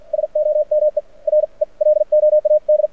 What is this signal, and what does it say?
Morse code, heard as a keyed tone, about 49 wpm, spaced as for 41 wpm, 600 Hz. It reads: SOME RELOAD
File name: archived / CRSnet